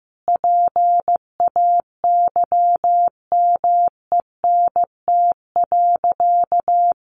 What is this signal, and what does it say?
Morse code, heard as a keyed tone, about 14 wpm, 700 Hz. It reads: PAYMENT.
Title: Sentences from Top 1400 Words 15wpm